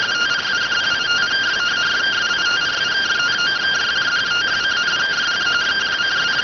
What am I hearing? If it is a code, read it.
Two-tone teleprinter signal, about 170 Hz shift.